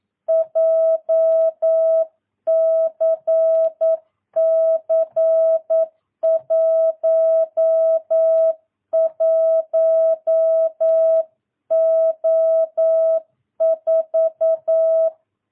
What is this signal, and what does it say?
Morse code, heard as a keyed tone, about 9 wpm, 650 Hz. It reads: JCC 11O4
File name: jh8cbh / 2022cw